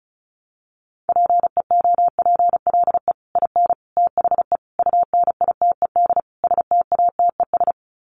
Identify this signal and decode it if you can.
Morse code, heard as a keyed tone, about 34 wpm, 700 Hz. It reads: PEOPLE IN THE UNITED STATES
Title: Sets of 5 Words Encore 35wpm